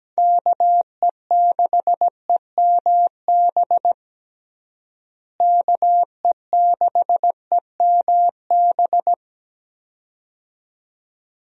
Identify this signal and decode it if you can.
Morse code, heard as a keyed tone, about 17 wpm, 700 Hz. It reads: KE6EMB KE6EMB
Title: USA Call Signs - 6 Characters - Repeated 2x - 17wpm